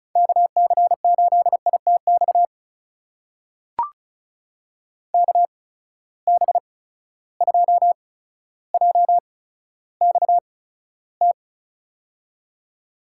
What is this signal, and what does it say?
Morse code, heard as a keyed tone, about 34 wpm, spaced as for 15 wpm, 700 Hz. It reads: KC8ITX KB2JXT
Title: USA Call Signs - 6 Characters - 8x Character Spacing - 35wpm